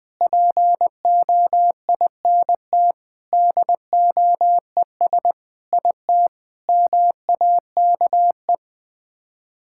Morse code, heard as a keyed tone, about 20 wpm, 700 Hz. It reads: POINT DOES IT MAKE
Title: Sets of 8 Words - 20wpm